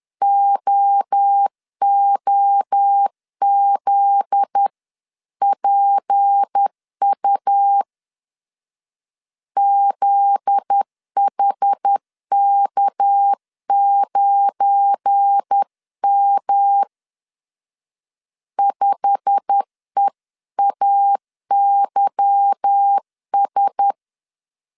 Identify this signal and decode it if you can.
Morse code, heard as a keyed tone, about 10 wpm, 800 Hz. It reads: OOZPU ZHK9M 5EAYS
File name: CW / DET / 1 9 1500C S A 10wpm 4511